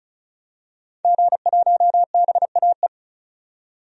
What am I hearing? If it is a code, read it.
Morse code, heard as a keyed tone, about 34 wpm, 700 Hz. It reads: G1BAE
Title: DXCC-100 Call Signs - 5 Characters - 35wpm